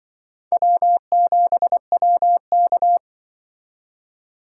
Morse code, heard as a keyed tone, about 23 wpm, 700 Hz. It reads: W7WK
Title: Speed Racing - USA 1x2 Call Signs 20wpm